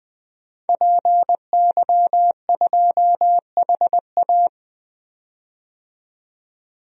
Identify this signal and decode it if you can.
Morse code, heard as a keyed tone, about 20 wpm, 700 Hz. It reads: PY2HA
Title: DXCC-100 Call Signs - 5 Characters - 20wpm